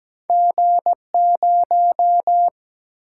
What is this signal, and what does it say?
Morse code, heard as a keyed tone, about 17 wpm, 700 Hz. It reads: G0
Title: Single-Letter Single-Number 17wpm